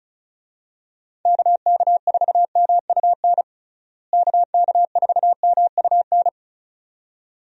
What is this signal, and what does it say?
Morse code, heard as a keyed tone, about 34 wpm, 700 Hz. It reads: KK4MUN KK4MUN
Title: USA Call Signs - 6 Characters - Repeated 2x - 35wpm